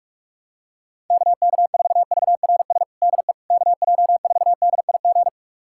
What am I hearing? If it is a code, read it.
Morse code, heard as a keyed tone, about 42 wpm, 700 Hz. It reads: KK4VRS DE KJ4DIG
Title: USA Callsign DE Callsign - 6 Characters - 45wpm